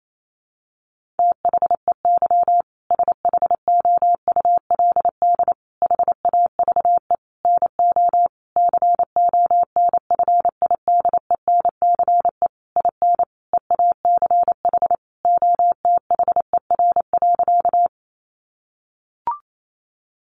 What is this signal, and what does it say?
Morse code, heard as a keyed tone, about 28 wpm, 700 Hz. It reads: THEY SHOULD HAVE NO CONFIDENCE IN EACH OTHER.
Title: Sentences from Top 1500 Words 28wpm